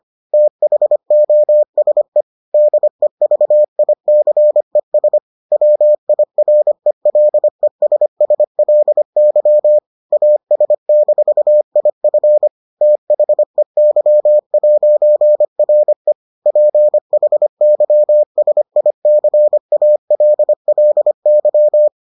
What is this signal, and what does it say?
Morse code, heard as a keyed tone, about 25 wpm, 600 Hz. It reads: THOSE DEVICES WIRELESSLY AS-IF THEY'RE PHYSICALLY